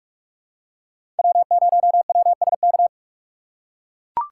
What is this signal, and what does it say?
Morse code, heard as a keyed tone, about 44 wpm, 700 Hz. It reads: W0WSK
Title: USA Call Signs - 5 Characters - 2x Character Spacing - 45wpm